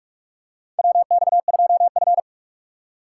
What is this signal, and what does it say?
Morse code, heard as a keyed tone, about 46 wpm, 700 Hz. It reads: WX2F
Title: USA Call Signs - 4 Characters - 3x Character Spacing - 45wpm